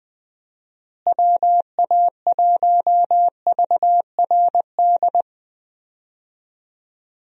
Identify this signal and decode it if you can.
Morse code, heard as a keyed tone, about 21 wpm, 700 Hz. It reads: WA1VRD
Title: USA Call Signs - 6 Characters - Repeated 3x - 20wpm